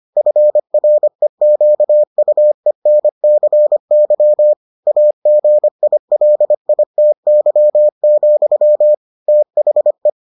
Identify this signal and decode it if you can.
Morse code, heard as a keyed tone, about 25 wpm, 600 Hz. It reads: FREQUENCY AGILITY, THE